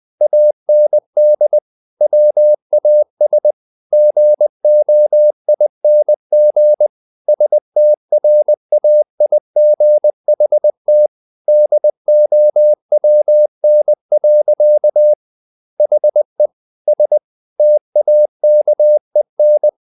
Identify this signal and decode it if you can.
Morse code, heard as a keyed tone, about 20 wpm, 600 Hz. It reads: AND WAS GOING STRAIGHT DOWN. HE S TAKEN
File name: the old man and the sea / the old man and the sea part 1 20wpm